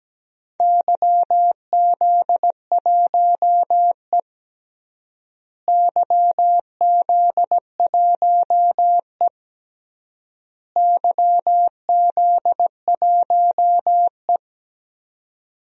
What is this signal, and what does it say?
Morse code, heard as a keyed tone, about 17 wpm, 700 Hz. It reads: YZ1E YZ1E YZ1E
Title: DXCC-100 Call Signs - 4 Characters - Repeated 3x - 17wpm